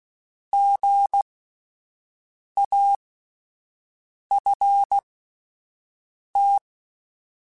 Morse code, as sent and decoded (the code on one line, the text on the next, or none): --. .- ..-. -
GAFT